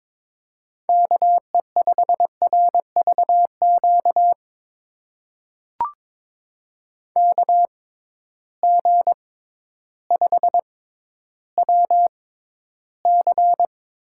Morse code, heard as a keyed tone, about 22 wpm, spaced as for 12 wpm, 700 Hz. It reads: KE5RVQ KG5WC
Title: USA Call Signs - 6 Characters - 6x Character Spacing - 22wpm